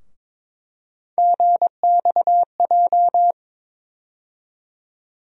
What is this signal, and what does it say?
Morse code, heard as a keyed tone, about 22 wpm, 700 Hz. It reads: GXJ
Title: USA Call Sign Postfixes - 3 Letters 22wpm